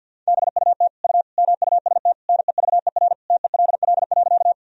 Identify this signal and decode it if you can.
Morse code, heard as a keyed tone, about 52 wpm, 700 Hz. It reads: BUT U MUST NEVER TELL.
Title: Sentences from Top 300 Words and Abbreviations 50wpm